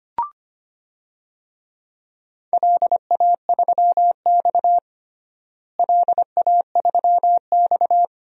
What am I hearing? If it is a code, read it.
Morse code, heard as a keyed tone, about 25 wpm, 700 Hz. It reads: LA3X LA3X
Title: DXCC-100 Call Signs - 4 Characters - Repeated 2x - 25wpm